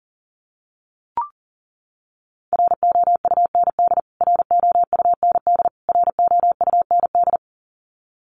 Morse code, heard as a keyed tone, about 40 wpm, 700 Hz. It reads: ROUND ROUND ROUND
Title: Top 500 Words Repeated 40wpm